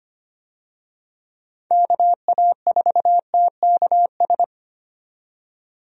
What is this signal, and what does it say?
Morse code, heard as a keyed tone, about 25 wpm, 700 Hz. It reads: KA4TKS
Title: USA Call Signs - 6 Characters - 25wpm